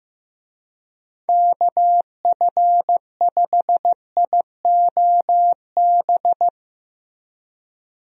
Morse code, heard as a keyed tone, about 15 wpm, 700 Hz. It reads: KF5IOB
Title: USA Call Signs - 6 Characters - Repeated 2x - 15wpm